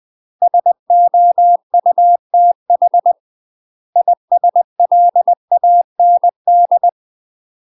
Morse code, heard as a keyed tone, about 20 wpm, 700 Hz. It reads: SOUTH ISLAND
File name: UK MASTER QSOs TRAINING / 20WPM / UK MASTER QSOs 0029